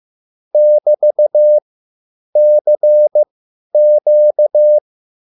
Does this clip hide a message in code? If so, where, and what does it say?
Yes: Morse code from 0.5 to 4.8 s, heard as a keyed tone, about 15 wpm, spaced as for 9 wpm, 600 Hz. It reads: <BT>CQ